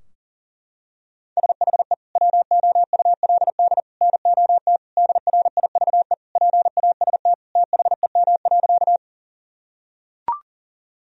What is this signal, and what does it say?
Morse code, heard as a keyed tone, about 40 wpm, 700 Hz. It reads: SHE WOULD NOT DRIVE PAST THEM.
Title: Sentences from Top 1000 Words - Review Part 3 - 40wpm